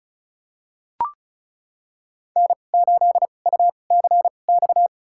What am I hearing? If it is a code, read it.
Morse code, heard as a keyed tone, about 36 wpm, spaced as for 25 wpm, 700 Hz. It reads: N8UCX
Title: USA Call Signs - 5 Characters - 2x Character Spacing - 35wpm